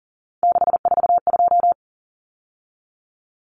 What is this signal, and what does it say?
Morse code, heard as a keyed tone, about 40 wpm, 700 Hz. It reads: 642
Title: Numbers - 3-Digit 40wpm